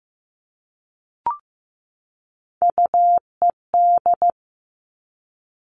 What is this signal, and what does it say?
Morse code, heard as a keyed tone, about 15 wpm, 700 Hz. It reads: UED
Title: Three Challenging Letter-Number Combinations 15wpm